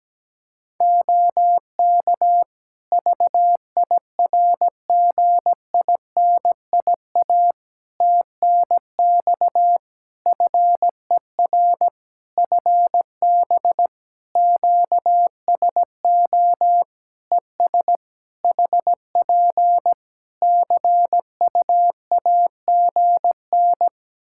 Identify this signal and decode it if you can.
Morse code, heard as a keyed tone, about 17 wpm, 700 Hz. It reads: OK VIRGINIA TNX FER FB QSO ES HP CUAGN
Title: Practice QSO - Exchange 4 - Ending - 17wpm